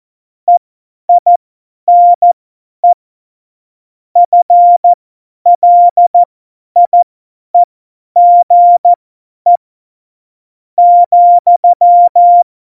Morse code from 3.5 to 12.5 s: FLIEGE ,